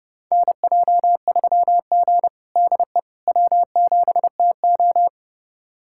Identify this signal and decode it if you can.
Morse code, heard as a keyed tone, about 32 wpm, 700 Hz. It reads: NJ3G DE W7TO
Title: USA Callsign DE Callsign - 4 Characters - 30wpm